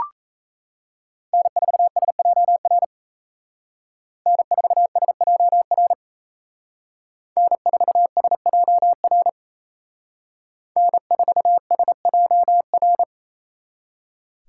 Morse code, heard as a keyed tone, about 33 wpm, 700 Hz. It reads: N4SJR N4SJR N4SJR N4SJR
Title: Speed Racing - USA Call Signs 28wpm